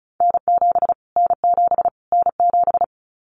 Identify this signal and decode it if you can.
Morse code, heard as a keyed tone, about 34 wpm, 700 Hz. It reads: N7 N7 N7
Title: Callsign Prefixes for Top 25 DXCC - Repeated 3x - 35wpm